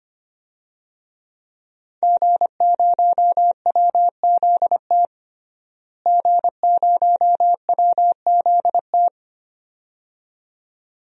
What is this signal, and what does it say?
Morse code, heard as a keyed tone, about 25 wpm, 700 Hz. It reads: G0WZT G0WZT
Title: DXCC-100 Call Signs - 5 Characters - Repeated 2x - 25wpm